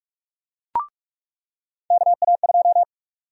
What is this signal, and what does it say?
Morse code, heard as a keyed tone, about 46 wpm, 700 Hz. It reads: KA2
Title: USA Call Sign Prefixes - 45wpm